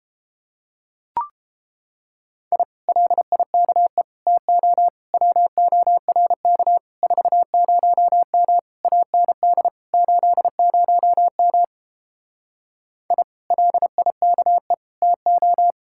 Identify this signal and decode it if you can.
Morse code, heard as a keyed tone, about 31 wpm, 700 Hz. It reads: I LIKE TO WORK 40M AND 80M I LIKE TO